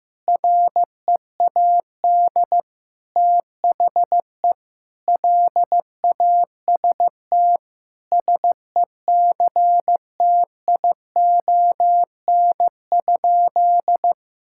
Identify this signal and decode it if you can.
Morse code, heard as a keyed tone, about 15 wpm, 700 Hz. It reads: READ THE LAST SECTION?
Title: Sentences from Top 1000 Words - Review Part 5 - 15wpm